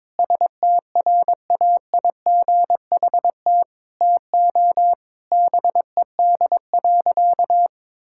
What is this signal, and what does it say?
Morse code, heard as a keyed tone, about 22 wpm, 700 Hz. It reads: STRAIGHT TO BED.
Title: Sentences from Top 1000 Words - Review Part 6 - 22wpm